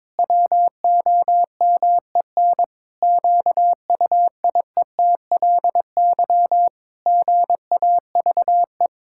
Morse code, heard as a keyed tone, about 22 wpm, 700 Hz. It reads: WOMEN QUIETLY GAVE
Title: Pangram Challenge v2 - 22wpm